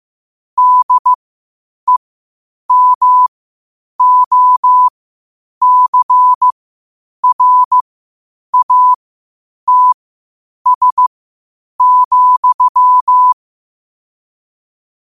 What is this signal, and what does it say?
Morse code, heard as a keyed tone, about 15 wpm, spaced as for 8 wpm, 1000 Hz. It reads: DEMOCRATS,
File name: qotdfiles / 8wpm